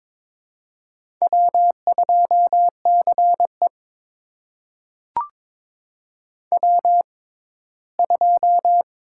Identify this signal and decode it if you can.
Morse code, heard as a keyed tone, about 22 wpm, spaced as for 15 wpm, 700 Hz. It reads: W2CE W2
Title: USA Call Signs - 4 Characters - 6x Character Spacing - 22wpm